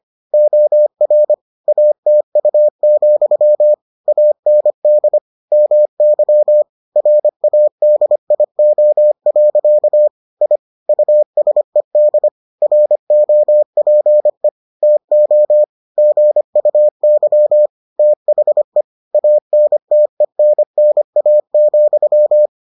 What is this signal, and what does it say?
Morse code, heard as a keyed tone, about 25 wpm, 600 Hz. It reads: OR ATU, AND MY RADIO. I USED ROPE TO GUY THE ANTENNA,